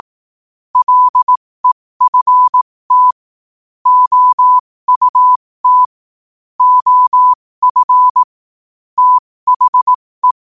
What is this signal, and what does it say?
Morse code, heard as a keyed tone, about 18 wpm, spaced as for 15 wpm, 1000 Hz. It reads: LEFT OUT OF THE
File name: qotdfiles / 15wpm